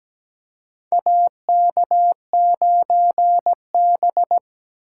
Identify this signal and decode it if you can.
Morse code, heard as a keyed tone, about 17 wpm, 700 Hz. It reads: AK9B
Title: USA Call Signs - 4 Characters - 2x Character Spacing - 17wpm